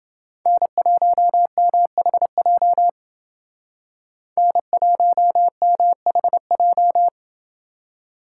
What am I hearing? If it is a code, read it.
Morse code, heard as a keyed tone, about 28 wpm, 700 Hz. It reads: N1MHJ N1MHJ